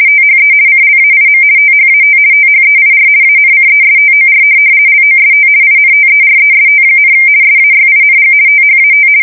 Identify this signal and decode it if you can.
RTTY, about 50 baud. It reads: THE CLAIM THAT THEIR ANCESTOR LOST INTEREST, SAYING THAT
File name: fsk170-50B / teletype-corp